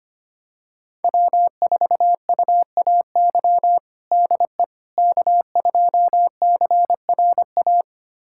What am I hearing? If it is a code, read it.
Morse code, heard as a keyed tone, about 25 wpm, 700 Hz. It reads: W4UAY DE K2CRA
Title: USA Callsign DE Callsign - 5 Characters - 25wpm